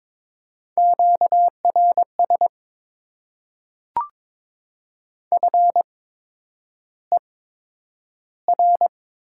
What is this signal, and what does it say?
Morse code, heard as a keyed tone, about 22 wpm, spaced as for 9 wpm, 700 Hz. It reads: QRS FER
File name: Top 90 QSO Elements - 8x Character Spacing - 22wpm